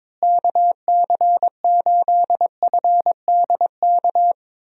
Morse code, heard as a keyed tone, about 22 wpm, 700 Hz. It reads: KC8FDK
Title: USA Call Signs - 6 Characters - Repeated 3x - 22wpm